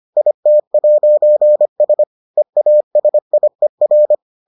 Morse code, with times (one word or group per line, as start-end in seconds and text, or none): 0.2-2.0 IT'S
2.4-4.1 EASIER